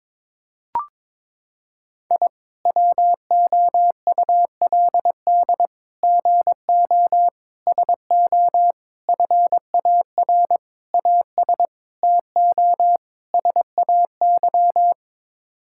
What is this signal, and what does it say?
Morse code, heard as a keyed tone, about 21 wpm, 700 Hz. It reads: I WOULD GO SO FAR AS TO SAY
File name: practice / 22wpm / Sets of 8 Words - 22wpm